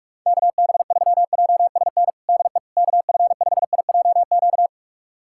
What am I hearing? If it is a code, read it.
Morse code, heard as a keyed tone, about 46 wpm, 700 Hz. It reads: KB3JSN DE KF5IJQ
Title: USA Callsign DE Callsign - 6 Characters - 45wpm